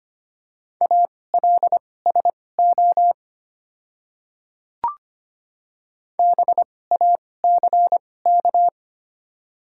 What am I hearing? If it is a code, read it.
Morse code, heard as a keyed tone, about 25 wpm, spaced as for 18 wpm, 700 Hz. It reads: ALSO BACK
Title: 2x Character Spacing - Top 200 Words - 25wpm